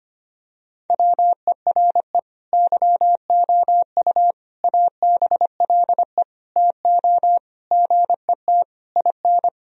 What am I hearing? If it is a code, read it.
Morse code, heard as a keyed tone, about 25 wpm, 700 Hz. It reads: WERE YOU ABLE TO GET IN
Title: Sentences from Top 1000 Words - Review Part 4 - 25wpm